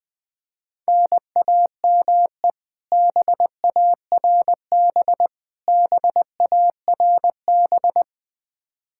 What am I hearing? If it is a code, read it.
Morse code, heard as a keyed tone, about 20 wpm, 700 Hz. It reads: NAME BARB BARB
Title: Practice QSO - Exchange 1 - Beginning - 20wpm